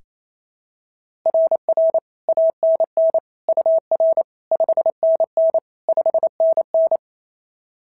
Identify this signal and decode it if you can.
Morse code, heard as a keyed tone, about 28 wpm, 690 Hz. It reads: RR ANN UR 5NN 5NN
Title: SOTA QSO Practice 28wpm